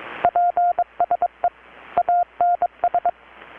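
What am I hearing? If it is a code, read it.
Morse code, heard as a keyed tone, about 23 wpm, 720 Hz. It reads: PSE ANS
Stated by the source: Morse from KPH's Night of Nights XX maritime radiotelegraphy event, received in Chiapas, Mexico about 3,800 km away on a Kenwood R-600 with a grounded 10 m long-wire antenna.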